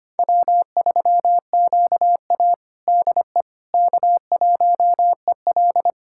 Morse code, heard as a keyed tone, about 25 wpm, 700 Hz. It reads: W3QA DE K1EL
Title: USA Callsign DE Callsign - 4 Characters - 25wpm